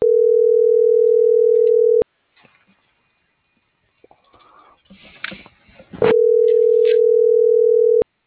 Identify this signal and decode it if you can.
Ringback tone (North America), 440 + 480 Hz.